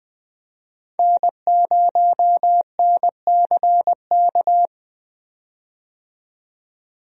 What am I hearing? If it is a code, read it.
Morse code, heard as a keyed tone, about 20 wpm, 700 Hz. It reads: N0NCK
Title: USA Call Signs - 5 Characters - 20wpm